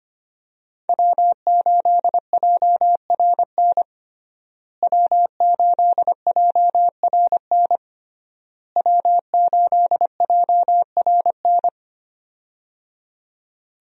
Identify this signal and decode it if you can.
Morse code, heard as a keyed tone, about 25 wpm, 700 Hz. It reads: W8JRN W8JRN W8JRN
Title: USA Call Signs - 5 Characters - Repeated 3x - 25wpm